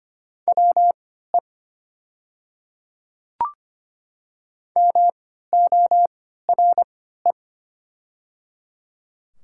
Morse code, heard as a keyed tone, about 25 wpm, spaced as for 14 wpm, 700 Hz. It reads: WE MORE
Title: 3x Character Spacing - Top 500 Words - 25wpm